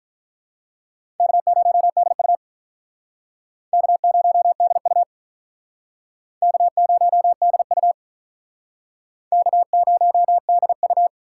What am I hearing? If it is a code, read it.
Morse code, heard as a keyed tone, about 44 wpm, 700 Hz. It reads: K0DU K0DU K0DU K0DU